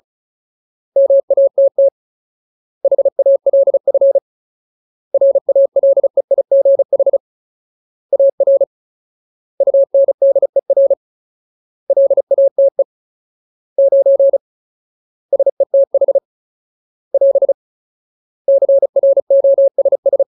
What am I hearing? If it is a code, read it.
Morse code, heard as a keyed tone, about 34 wpm, 550 Hz. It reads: MATT HALF RALEIGH AR UNDER LATE 9 SETH <AS> CROSS
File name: Ultimate Word List / 35WPM Ultimate